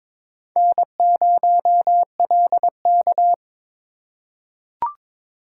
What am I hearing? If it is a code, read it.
Morse code, heard as a keyed tone, about 22 wpm, 700 Hz. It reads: N0LK
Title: USA Call Signs - 4x Character Spacing - 22wpm